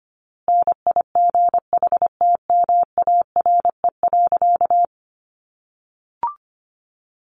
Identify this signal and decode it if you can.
Morse code, heard as a keyed tone, about 25 wpm, 700 Hz. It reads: NIGHTMARE.
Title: A Sound Of Thunder 25wpm